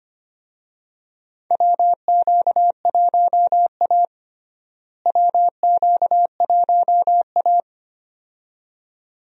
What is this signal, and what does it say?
Morse code, heard as a keyed tone, about 25 wpm, 700 Hz. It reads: WQ1A WQ1A
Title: USA Call Signs - 4 Characters - Repeated 2x - 25wpm